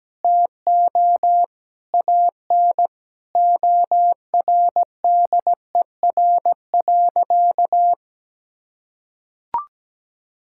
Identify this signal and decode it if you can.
Morse code, heard as a keyed tone, about 17 wpm, 700 Hz. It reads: TO AN ORDER.
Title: Sentences from Top 1600 Words 17wpm